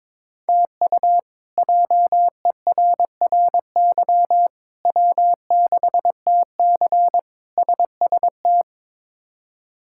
Morse code, heard as a keyed tone, about 22 wpm, 700 Hz. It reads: TU JERRY W6TC SST
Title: K1USN SST QSO Practice - 22wpm